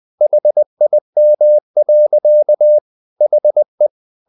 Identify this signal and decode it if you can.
Morse code, heard as a keyed tone, about 20 wpm, 600 Hz. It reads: HIM. HE
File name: the old man and the sea part 2 20wpm